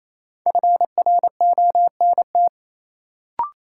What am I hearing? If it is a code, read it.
Morse code, heard as a keyed tone, about 28 wpm, 700 Hz. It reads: FRONT
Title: Sentences from Top 500 Words - New Word Drill - 28wpm